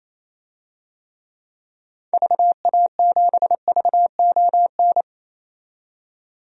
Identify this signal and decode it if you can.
Morse code, heard as a keyed tone, about 28 wpm, 700 Hz. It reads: VA7VON
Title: DXCC-100 Call Signs - 6 Characters - 28wpm